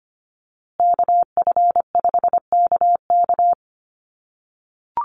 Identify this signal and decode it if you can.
Morse code, heard as a keyed tone, about 25 wpm, 700 Hz. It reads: KF5KK
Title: USA Call Signs Repeated 3x 25wpm